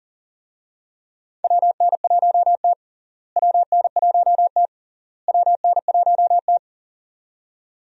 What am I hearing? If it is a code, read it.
Morse code, heard as a keyed tone, about 40 wpm, 700 Hz. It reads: WN1T WN1T WN1T
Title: USA Call Signs - 4 Characters - Repeated 3x - 40wpm